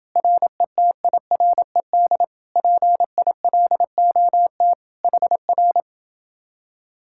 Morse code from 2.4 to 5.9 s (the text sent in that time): PILOT HR